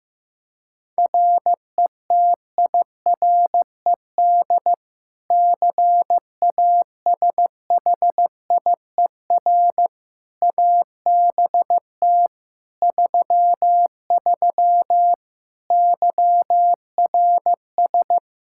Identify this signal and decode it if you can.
Morse code, heard as a keyed tone, about 15 wpm, 700 Hz. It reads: RETIRED CASHIER ABT 33 YRS
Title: Practice QSO - Exchange 3 - More Info - 15wpm